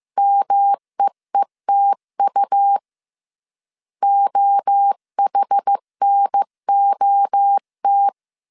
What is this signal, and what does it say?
Morse code, heard as a keyed tone, about 15 wpm, 800 Hz. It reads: MEETU OHNOT
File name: CW / DET / 1 3 1500C S A 14wpm 2447